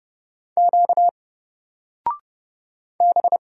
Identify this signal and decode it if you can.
Morse code, heard as a keyed tone, about 30 wpm, 700 Hz. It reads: Q B
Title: Single Character (Letter:Number:Punctuation:Prosign) 30wpm